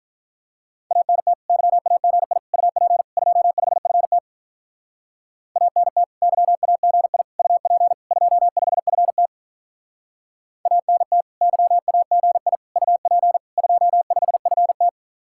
Morse code, heard as a keyed tone, about 44 wpm, 700 Hz. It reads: ANT YAGI UP 25FT ANT YAGI UP 25FT ANT YAGI UP 25FT